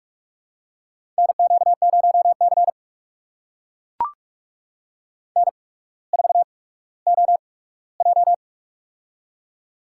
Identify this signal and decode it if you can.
Morse code, heard as a keyed tone, about 44 wpm, spaced as for 19 wpm, 700 Hz. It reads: NQ0C N4OJ
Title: USA Call Signs - 4 Characters - 8x Character Spacing - 45wpm